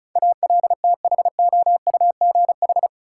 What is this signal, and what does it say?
Morse code, heard as a keyed tone, about 36 wpm, 700 Hz. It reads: ALTHOUGH
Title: Sentences from Top 600 Words - New Word Drill - 35wpm